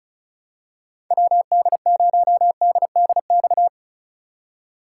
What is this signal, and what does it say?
Morse code, heard as a keyed tone, about 36 wpm, 700 Hz. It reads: WD0DDX
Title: USA Call Signs - 6 Characters - 3x Character Spacing - 35wpm